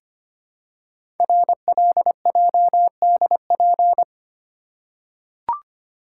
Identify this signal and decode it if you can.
Morse code, heard as a keyed tone, about 26 wpm, 700 Hz. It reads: RLJDP
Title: Code Groups - 5 Letters - 25wpm